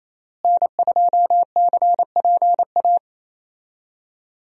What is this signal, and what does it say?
Morse code, heard as a keyed tone, about 28 wpm, 700 Hz. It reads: N2CPA
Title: USA Call Signs - 5 Characters - Repeated 3x - 28wpm